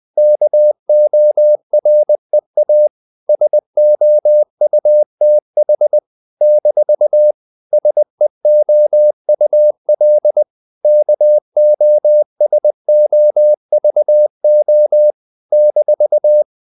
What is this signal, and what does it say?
Morse code, heard as a keyed tone, about 20 wpm, 600 Hz. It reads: KOREA SOUTH - SEOUL KOSOVO -
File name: CW / Practice / NationalCapitals20